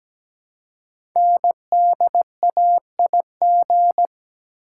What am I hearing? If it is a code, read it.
Morse code, heard as a keyed tone, about 17 wpm, 700 Hz. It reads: NDAIG
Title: Code Groups - 5 Letters - 17wpm